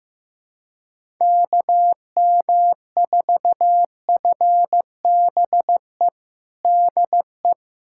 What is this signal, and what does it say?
Morse code, heard as a keyed tone, about 15 wpm, 700 Hz. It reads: KM4FBE DE
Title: USA Callsign DE Callsign - 6 Characters - 15wpm